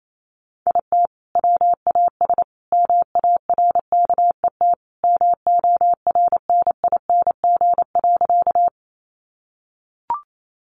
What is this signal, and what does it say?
Morse code, heard as a keyed tone, about 28 wpm, 700 Hz. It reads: IT WAS MARKET MORNING.
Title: Sentences from Top 600 Words 28wpm